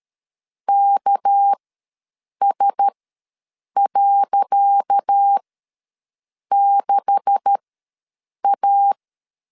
Morse code, heard as a keyed tone, about 13 wpm, spaced as for 7 wpm, 800 Hz. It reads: KS.6A
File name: LEZIONI / 10 SIGNS / 10C DOT ws 12wpm 3548